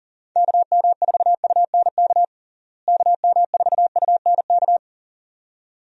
Morse code, heard as a keyed tone, about 40 wpm, 700 Hz. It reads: KM4UNK KM4UNK
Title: USA Call Signs - 6 Characters - Repeated 2x - 40wpm